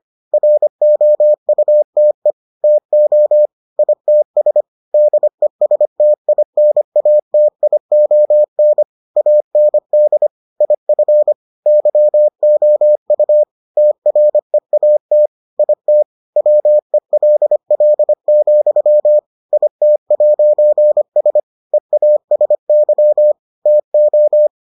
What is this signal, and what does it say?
Morse code, heard as a keyed tone, about 25 wpm, 600 Hz. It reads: ROUTE TO ITS DESTINATION AND IF YOU TREAT IT WELL, IT'S EASY TO